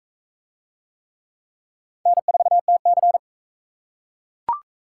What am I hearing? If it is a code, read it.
Morse code, heard as a keyed tone, about 39 wpm, 700 Hz. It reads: N4TC